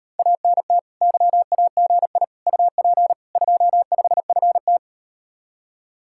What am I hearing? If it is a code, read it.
Morse code, heard as a keyed tone, about 38 wpm, 700 Hz. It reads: ANT YAGI UP 25FT